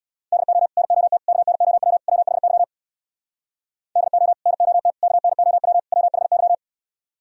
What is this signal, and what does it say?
Morse code, heard as a keyed tone, about 64 wpm, 700 Hz. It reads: DO NOT KNOW WHY DO NOT KNOW WHY